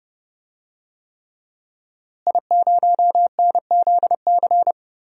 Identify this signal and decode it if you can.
Morse code, heard as a keyed tone, about 30 wpm, 700 Hz. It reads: I0NZC
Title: DXCC-100 Call Signs - 5 Characters - Repeated 3x - 30wpm